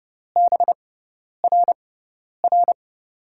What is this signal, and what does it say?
Morse code, heard as a keyed tone, about 32 wpm, spaced as for 10 wpm, 700 Hz. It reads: BRR
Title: USA Call Signs - 6 Characters - 6x Character Spacing - 30wpm